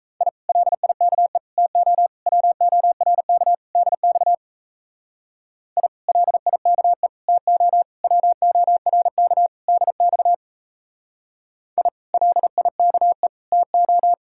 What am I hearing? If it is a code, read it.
Morse code, heard as a keyed tone, about 36 wpm, 700 Hz. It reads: I LIKE TO WORK DX I LIKE TO WORK DX I LIKE TO